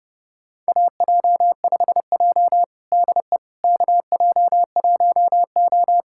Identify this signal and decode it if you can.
Morse code, heard as a keyed tone, about 30 wpm, 700 Hz. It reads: AJ5J DE KJ1O
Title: USA Callsign DE Callsign - 4 Characters - 30wpm